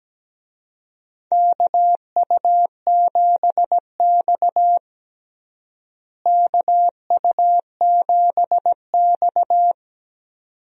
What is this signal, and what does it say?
Morse code, heard as a keyed tone, about 17 wpm, 700 Hz. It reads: KU7X KU7X
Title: USA Call Signs - 4 Characters - Repeated 2x - 17wpm